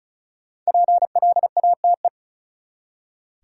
Morse code, heard as a keyed tone, about 36 wpm, 700 Hz. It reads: PLATE
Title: Five Letter Words - 35wpm